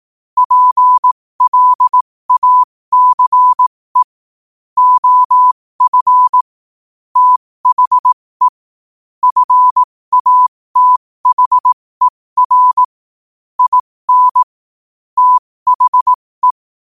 Morse code, heard as a keyed tone, about 18 wpm, spaced as for 15 wpm, 1000 Hz. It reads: PLACE OF THE FATHER IN THE
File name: 15wpm